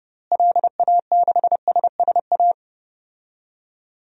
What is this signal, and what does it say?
Morse code, heard as a keyed tone, about 32 wpm, 700 Hz. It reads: LA6SSA
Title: DXCC-100 Call Signs - 6 Characters - Repeated 2x - 30wpm